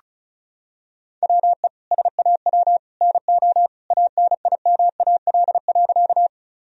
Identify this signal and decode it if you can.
Morse code, heard as a keyed tone, about 34 wpm, 700 Hz. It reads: WE SAW NO ANIMAL.
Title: Sentences from Top 1000 Words - Review Part 1 - 35wpm